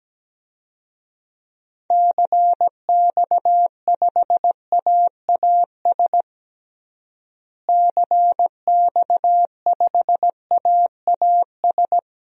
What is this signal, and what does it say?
Morse code, heard as a keyed tone, about 17 wpm, 700 Hz. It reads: CX5AAS CX5AAS
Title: DXCC-100 Call Signs - 6 Characters - Repeated 3x - 17wpm